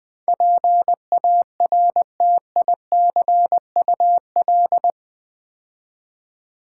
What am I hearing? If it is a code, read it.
Morse code, heard as a keyed tone, about 20 wpm, 700 Hz. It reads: PARTICUL
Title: Build A Word - 4 Letters At A Time - 12-Letter Words - 20wpm